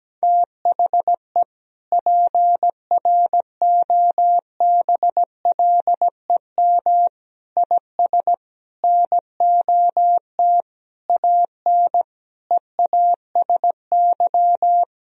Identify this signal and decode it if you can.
Morse code, heard as a keyed tone, about 17 wpm, 700 Hz. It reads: THE PROBLEM IS NOT AN EASY
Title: Sentences from Top 1000 Words - Review Part 2 - 17wpm